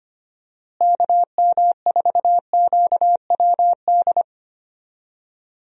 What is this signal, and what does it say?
Morse code, heard as a keyed tone, about 25 wpm, 700 Hz. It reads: KM4QWD
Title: USA Call Signs - 6 Characters - Repeated 3x - 25wpm